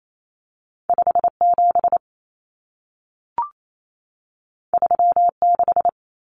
Morse code, heard as a keyed tone, about 28 wpm, 700 Hz. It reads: 57 36
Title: Common Cut Numbers - 2-Digit 28wpm